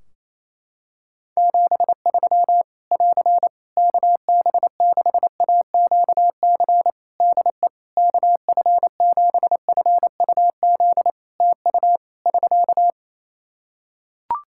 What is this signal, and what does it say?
Morse code, heard as a keyed tone, about 28 wpm, 700 Hz. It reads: 73 <AR> KB6AQC DE KF7FUZ TU <SK>
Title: Practice QSO - Exchange 4 - Ending - 28wpm